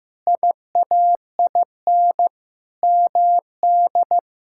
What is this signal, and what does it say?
Morse code, heard as a keyed tone, about 16 wpm, 700 Hz. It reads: IAIN MD
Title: K1USN SST QSO Practice - 15wpm